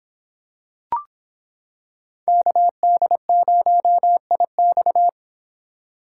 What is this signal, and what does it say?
Morse code, heard as a keyed tone, about 26 wpm, 700 Hz. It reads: KD0IX